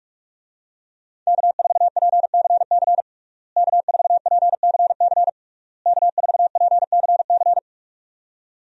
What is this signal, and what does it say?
Morse code, heard as a keyed tone, about 47 wpm, 700 Hz. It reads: K4PCC K4PCC K4PCC
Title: USA Call Signs - 5 Characters - Repeated 3x - 45wpm